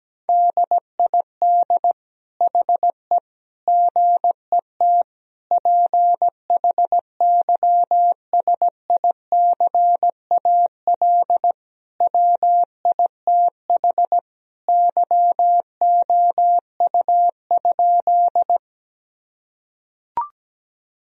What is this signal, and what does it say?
Morse code, heard as a keyed tone, about 17 wpm, 700 Hz. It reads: DID HE GET PHYSICAL WITH YOU?
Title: Sentences from Top 1000 Words - Review Part 4 - 17wpm